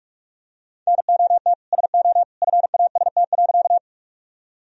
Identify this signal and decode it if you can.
Morse code, heard as a keyed tone, about 46 wpm, 700 Hz. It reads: NOT SO FAST.
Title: Sentences from Top 1000 Words - Review Part 2 - 45wpm